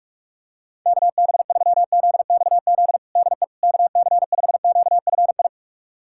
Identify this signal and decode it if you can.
Morse code, heard as a keyed tone, about 46 wpm, 700 Hz. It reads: KB3ZXZ DE KC5QFI
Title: USA Callsign DE Callsign - 6 Characters - 45wpm